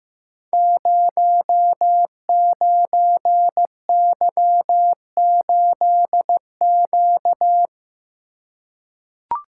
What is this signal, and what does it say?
Morse code, heard as a keyed tone, about 15 wpm, 700 Hz. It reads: 09Y8Q
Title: Code Groups - 5 Letter:Numbers - 15wpm